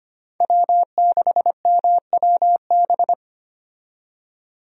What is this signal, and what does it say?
Morse code, heard as a keyed tone, about 25 wpm, 700 Hz. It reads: W6MWB
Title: USA Call Signs - 5 Characters - 25wpm